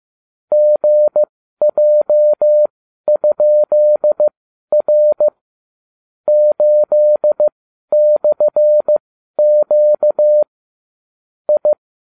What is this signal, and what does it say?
Morse code, heard as a keyed tone, about 15 wpm, spaced as for 12 wpm, 600 Hz. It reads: GJ?R 8/Q I
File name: cw / ubacw35 3